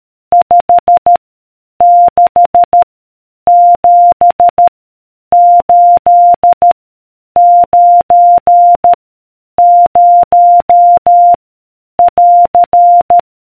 Morse code, heard as a keyed tone, about 13 wpm, 700 Hz. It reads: 5 6 7 8 9 0 <AR>